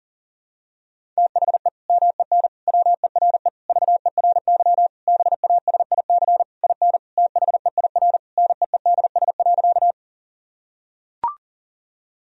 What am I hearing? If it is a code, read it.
Morse code, heard as a keyed tone, about 40 wpm, 700 Hz. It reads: THE MEN WERE VERY BASIC IN THEIR NEEDS.
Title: Sentences from Top 1000 Words - Review Part 3 - 40wpm